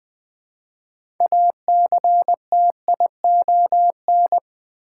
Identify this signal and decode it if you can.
Morse code, heard as a keyed tone, about 20 wpm, 700 Hz. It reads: ACTION
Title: Build A Word - 3 Letters At A Time - 6-Letter Words - 20wpm